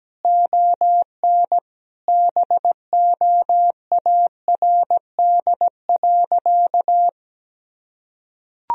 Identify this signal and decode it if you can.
Morse code, heard as a keyed tone, about 17 wpm, 700 Hz. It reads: ON BOARD.
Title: Sentences from Top 1000 Words - Review Part 4 - 17wpm